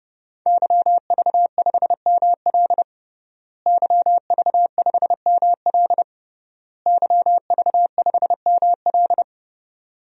Morse code, heard as a keyed tone, about 32 wpm, 700 Hz. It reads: YV5ML YV5ML YV5ML
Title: DXCC-100 Call Signs - 5 Characters - Repeated 3x - 30wpm